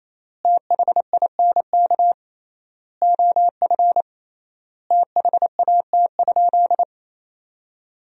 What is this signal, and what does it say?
Morse code, heard as a keyed tone, about 28 wpm, 700 Hz. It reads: THINK OF THAT?
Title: 3x Word-Spacing - Sentences from Top 200 Words - 28wpm